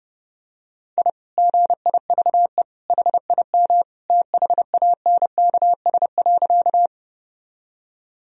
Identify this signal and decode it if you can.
Morse code, heard as a keyed tone, about 32 wpm, 700 Hz. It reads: I GIVE HIM THANKS.
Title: Sentences from Top 1000 Words - Review Part 3 - 30wpm